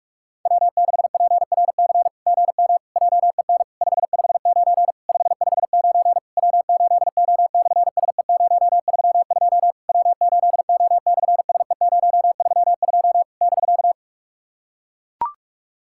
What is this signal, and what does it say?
Morse code, heard as a keyed tone, about 44 wpm, 690 Hz. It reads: W6PRC GM JEN 559 559 W8O/SE032 W8O/SE032 <BK>
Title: SOTA QSO Practice 45wpm